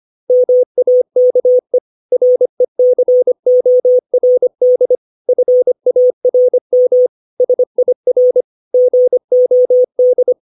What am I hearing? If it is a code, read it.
Morse code, heard as a keyed tone, about 25 wpm, 500 Hz. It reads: MAKE RECORD FARM SIR GOD
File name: 632 words / 632 Words 25WPM